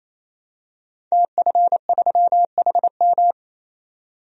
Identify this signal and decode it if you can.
Morse code, heard as a keyed tone, about 28 wpm, 700 Hz. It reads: TF3HM
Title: DXCC-100 Call Signs - 5 Characters - 28wpm